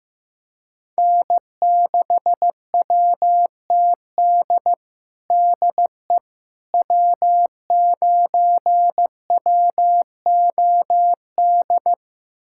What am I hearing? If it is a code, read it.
Morse code, heard as a keyed tone, about 15 wpm, 700 Hz. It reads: N6WTD DE W9WOD
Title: USA Callsign DE Callsign - 5 Characters - 15wpm